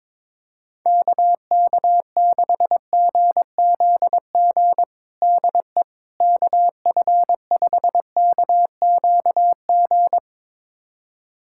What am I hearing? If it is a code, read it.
Morse code, heard as a keyed tone, about 22 wpm, 700 Hz. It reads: KK6GZG DE KF5KQG
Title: USA Callsign DE Callsign - 6 Characters - 22wpm